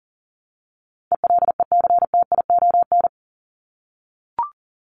Morse code, heard as a keyed tone, about 37 wpm, 700 Hz. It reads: ELECTION
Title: Top 1251-1500 Words 40wpm